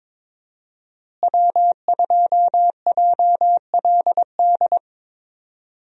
Morse code, heard as a keyed tone, about 21 wpm, 700 Hz. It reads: W2JLD
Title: USA Call Signs - 5 Characters - Repeated 2x - 22wpm